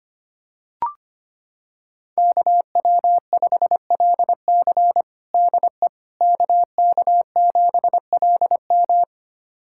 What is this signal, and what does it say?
Morse code, heard as a keyed tone, about 25 wpm, 700 Hz. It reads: KW5LC DE KK7LM
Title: USA Callsign DE Callsign - 5 Characters - 25wpm